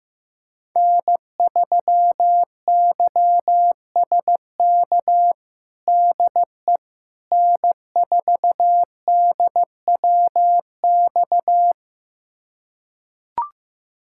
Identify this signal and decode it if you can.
Morse code, heard as a keyed tone, about 15 wpm, 700 Hz. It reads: N3YSK DE N4DWX
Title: USA Callsign DE Callsign - 5 Characters - 15wpm